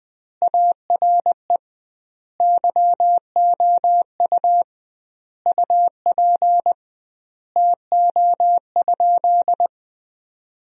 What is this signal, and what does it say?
Morse code, heard as a keyed tone, about 20 wpm, 700 Hz. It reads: ARE YOU UP TO?
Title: 2x Word-Spacing - Sentences from Top 100 Words - 20wpm